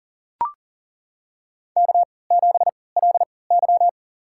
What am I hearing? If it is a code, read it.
Morse code, heard as a keyed tone, about 40 wpm, spaced as for 23 wpm, 700 Hz. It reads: K7LY
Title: USA Call Signs - 4 Characters - 3x Character Spacing - 40wpm